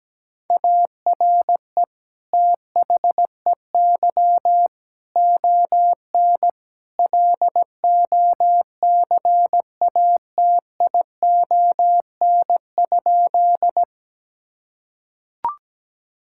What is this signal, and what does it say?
Morse code, heard as a keyed tone, about 17 wpm, 700 Hz. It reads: ARE THEY ON LOCATION?
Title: Sentences from Top 1000 Words - Review Part 6 - 17wpm